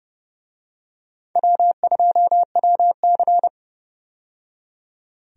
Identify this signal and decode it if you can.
Morse code, heard as a keyed tone, about 32 wpm, 700 Hz. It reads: W2WC
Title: USA Call Signs - 4 Characters - 30wpm